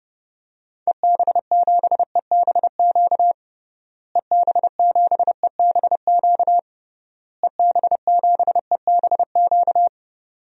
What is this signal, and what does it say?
Morse code, heard as a keyed tone, about 30 wpm, 700 Hz. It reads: EB7EBQ EB7EBQ EB7EBQ
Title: DXCC-100 Call Signs - 6 Characters - Repeated 3x - 30wpm